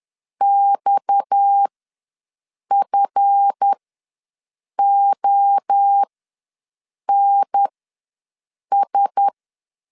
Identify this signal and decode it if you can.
Morse code, heard as a keyed tone, about 10 wpm, spaced as for 6 wpm, 800 Hz. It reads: XFONS
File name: LEZIONI / 9 2 ws 10wpm 4130